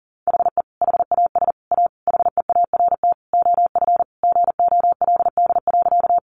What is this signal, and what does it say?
Morse code, heard as a keyed tone, about 40 wpm, 700 Hz. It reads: HE HAS A HEART OF GOLD.
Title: Sentences from Top 1200 Words 40wpm